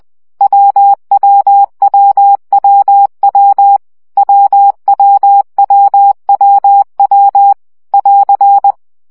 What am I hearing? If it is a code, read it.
Morse code, heard as a keyed tone, about 20 wpm, 800 Hz. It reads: WWWWW WWWWW <AR>
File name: CW gyakorlas w 17 baud